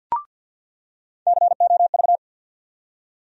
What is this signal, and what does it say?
Morse code, heard as a keyed tone, about 48 wpm, 700 Hz. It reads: COV
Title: USA Call Sign Postfixes - 3 Letters 50wpm